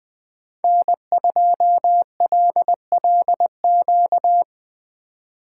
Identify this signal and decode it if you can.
Morse code, heard as a keyed tone, about 19 wpm, 700 Hz. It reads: N2LLQ